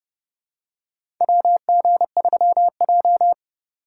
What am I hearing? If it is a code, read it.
Morse code, heard as a keyed tone, about 29 wpm, 700 Hz. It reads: WG3J
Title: USA Call Signs - 6x Character Spacing - 30wpm